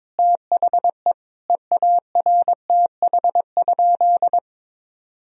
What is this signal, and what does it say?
Morse code, heard as a keyed tone, about 22 wpm, 700 Hz. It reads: THE EARTH?
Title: Sentences from Top 1000 Words - Review Part 3 - 22wpm